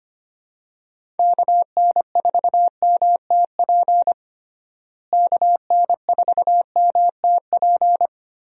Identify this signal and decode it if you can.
Morse code, heard as a keyed tone, about 25 wpm, 700 Hz. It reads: KN4MTP KN4MTP
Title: USA Call Signs - 6 Characters - Repeated 3x - 25wpm